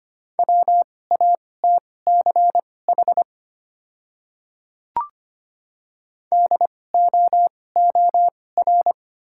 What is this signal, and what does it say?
Morse code, heard as a keyed tone, about 25 wpm, spaced as for 18 wpm, 700 Hz. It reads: WATCH DOOR
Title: 2x Character Spacing - Top 300 Words - 25wpm